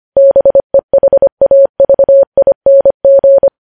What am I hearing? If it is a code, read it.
Morse code, heard as a keyed tone, about 25 wpm, 550 Hz. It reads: BEHAVING
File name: podcast / News headlines-1x-25x25